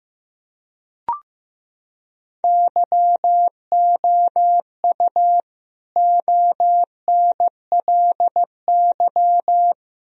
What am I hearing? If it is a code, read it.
Morse code, heard as a keyed tone, about 15 wpm, 700 Hz. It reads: YOU ONLY
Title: Sentences from Top 1000 Words - Review Part 2 - 15wpm